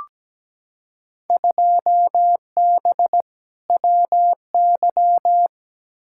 Morse code, heard as a keyed tone, about 17 wpm, 700 Hz. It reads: 2B WY
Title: Field Day Exchanges Encore - 17wpm